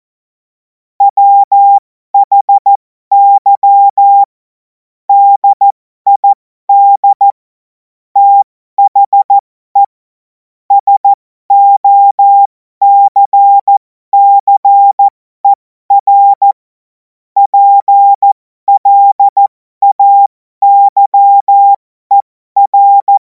Morse code, heard as a keyed tone, about 14 wpm, spaced as for 12 wpm, 800 Hz. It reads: WHY DID THE SOCCER PLAYER
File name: Morsekurs / Witze